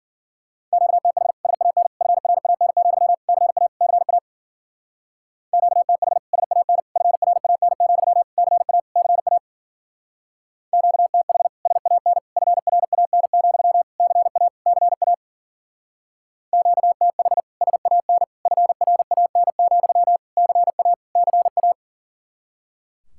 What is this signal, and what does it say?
Morse code, heard as a keyed tone, about 49 wpm, 700 Hz. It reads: QTH SAN FRAN, CA CA QTH SAN FRAN, CA CA QTH SAN FRAN, CA CA QTH SAN FRAN, CA CA
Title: Speed-Racing Common QSO Phrases - 40wpm